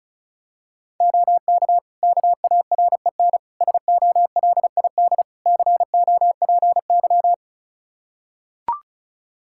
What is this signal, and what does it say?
Morse code, heard as a keyed tone, about 36 wpm, 700 Hz. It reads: OK KAREN SOLID COPY
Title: Practice QSO - Exchange 3 - More Info - 35wpm